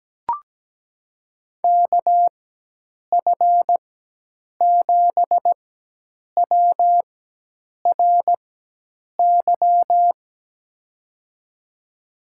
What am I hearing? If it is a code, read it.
Morse code, heard as a keyed tone, about 17 wpm, spaced as for 8 wpm, 700 Hz. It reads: KF7WRY
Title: USA Call Signs - 6 Characters - 4x Character Spacing - 17wpm